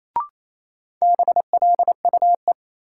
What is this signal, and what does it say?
Morse code, heard as a keyed tone, about 28 wpm, 700 Hz. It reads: BLUE
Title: Sentences from Top 1200 Words - New Word Drill Alt - 28wpm